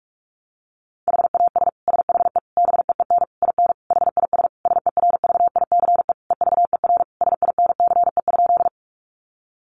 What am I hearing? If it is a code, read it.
Morse code, heard as a keyed tone, about 44 wpm, 700 Hz. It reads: HAS SHE BEEN IN HIS SERVICE EVER SINCE?
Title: Sentences from Top 700 Words 45wpm